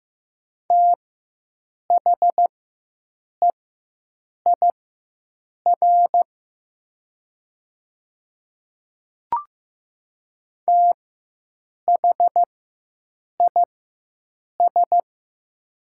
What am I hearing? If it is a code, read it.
Morse code, heard as a keyed tone, about 15 wpm, spaced as for 7 wpm, 700 Hz. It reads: THEIR THIS
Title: 4x Character Spacing - Top 100 Words - 15wpm